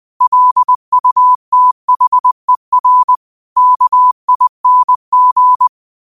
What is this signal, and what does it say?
Morse code, heard as a keyed tone, about 20 wpm, 1000 Hz. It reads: LUTHER KING